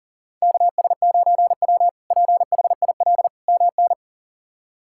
Morse code, heard as a keyed tone, about 40 wpm, 700 Hz. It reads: KS9W PHIL MN
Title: K1USN SST QSO Practice - 40wpm